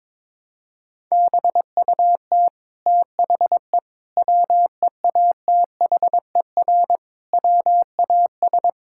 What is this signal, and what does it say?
Morse code, heard as a keyed tone, about 22 wpm, 700 Hz. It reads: BUT THE WEATHER WAS
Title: Sentences from Top 1000 Words - Review Part 5 - 22wpm